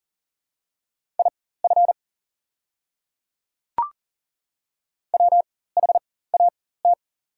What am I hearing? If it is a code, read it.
Morse code, heard as a keyed tone, about 40 wpm, spaced as for 19 wpm, 700 Hz. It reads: IF WHAT
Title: 4x Character Spacing - Top 100 Words - 40wpm